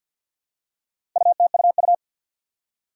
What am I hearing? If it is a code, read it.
Morse code, heard as a keyed tone, about 53 wpm, 700 Hz. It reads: UTUU
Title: Four Challenging Letter-Number Combinations 50wpm